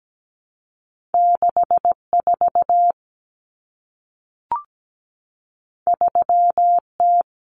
Morse code, heard as a keyed tone, about 17 wpm, 700 Hz. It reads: 64 3T
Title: Common Cut Numbers - 2-Digit 17wpm